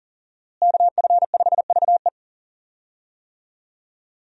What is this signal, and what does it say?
Morse code, heard as a keyed tone, about 37 wpm, 700 Hz. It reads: KF5VE
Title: USA Call Signs - 5 Characters - Repeated 2x - 40wpm